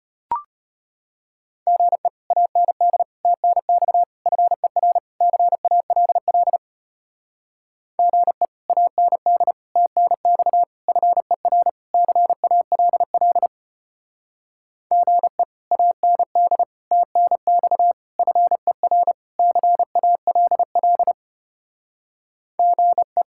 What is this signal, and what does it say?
Morse code, heard as a keyed tone, about 32 wpm, 700 Hz. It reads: GE AND TNX FER CALL GE AND TNX FER CALL GE AND TNX FER CALL GE